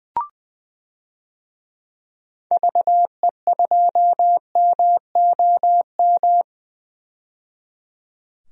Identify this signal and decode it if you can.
Morse code, heard as a keyed tone, about 20 wpm, 700 Hz. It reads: VE2MOM
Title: DXCC-100 Call Signs - 6 Characters - 20wpm